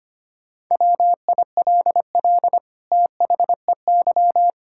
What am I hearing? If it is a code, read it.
Morse code, heard as a keyed tone, about 25 wpm, 700 Hz. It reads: WILL THEY
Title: Sentences from Top 1000 Words - Review Part 5 - 25wpm